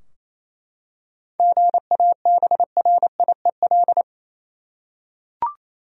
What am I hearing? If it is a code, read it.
Morse code, heard as a keyed tone, about 28 wpm, 700 Hz. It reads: GABRIEL
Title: USA Common Names - 28wpm